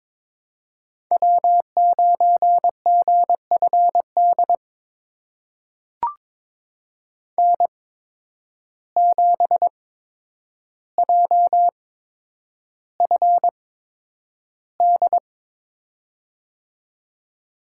Morse code, heard as a keyed tone, about 22 wpm, spaced as for 9 wpm, 700 Hz. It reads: W9GFD N7JFD
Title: USA Call Signs - 8x Character Spacing - 22wpm